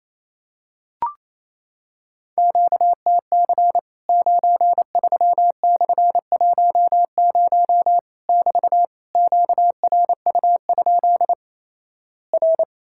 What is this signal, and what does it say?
Morse code, heard as a keyed tone, about 27 wpm, 700 Hz. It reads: QTC 93/10 <BT> QRU? R
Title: Worked All Europe DX Contest - QTC Exchange - 28wpm